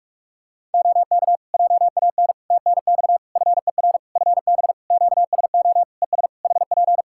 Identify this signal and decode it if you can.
Morse code, heard as a keyed tone, about 46 wpm, 700 Hz. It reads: OK JAN TNX FER FB QSO ES HP
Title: K1USN SST QSO Practice 45wpm